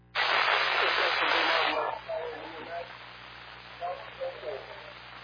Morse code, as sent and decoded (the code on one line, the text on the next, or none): none